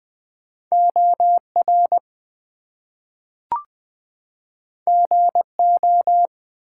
Morse code, heard as a keyed tone, about 20 wpm, 700 Hz. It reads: OR GO
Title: Two Letter Words - 20wpm